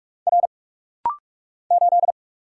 Morse code, heard as a keyed tone, about 44 wpm, 690 Hz. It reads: R 8
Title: Single Letter-Number - Rapid-Fire 45wpm